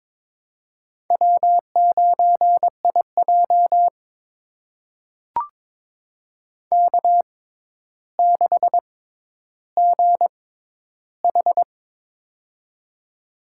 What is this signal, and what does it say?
Morse code, heard as a keyed tone, about 22 wpm, spaced as for 11 wpm, 700 Hz. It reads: W9IJ K6GH
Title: USA Call Signs - 6x Character Spacing - 22wpm